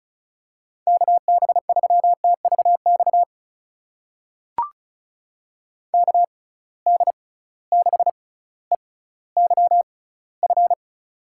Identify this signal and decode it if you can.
Morse code, heard as a keyed tone, about 36 wpm, spaced as for 18 wpm, 700 Hz. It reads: KB3TVX KD6EYF
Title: USA Call Signs - 6x Character Spacing - 35wpm